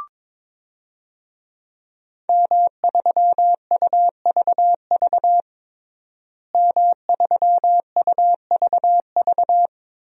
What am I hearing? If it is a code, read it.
Morse code, heard as a keyed tone, about 22 wpm, 700 Hz. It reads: M3UVV M3UVV
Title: DXCC-100 Call Signs - 5 Characters - Repeated 3x - 22wpm